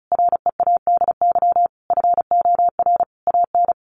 Morse code, heard as a keyed tone, about 34 wpm, 700 Hz. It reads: READY FOR AN
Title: Sentences from Top 1500 Words 35wpm